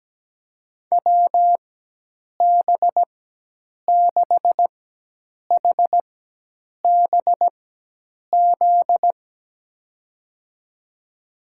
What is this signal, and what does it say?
Morse code, heard as a keyed tone, about 17 wpm, spaced as for 8 wpm, 700 Hz. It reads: WB6HBZ
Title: USA Call Signs - 6 Characters - 4x Character Spacing - 17wpm